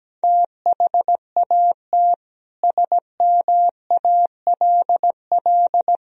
Morse code, heard as a keyed tone, about 17 wpm, 700 Hz. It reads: THAT SMALL
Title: Sentences from Top 1000 Words - Review Part 4 - 17wpm